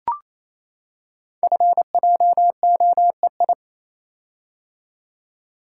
Morse code, heard as a keyed tone, about 27 wpm, 700 Hz. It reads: FJOEI
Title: Code Groups - 5 Letters - 28wpm